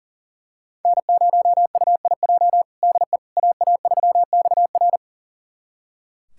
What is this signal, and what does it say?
Morse code, heard as a keyed tone, about 40 wpm, 700 Hz. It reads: N0UIJ DE AA3XR
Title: USA Callsign DE Callsign - 5 Characters - 40wpm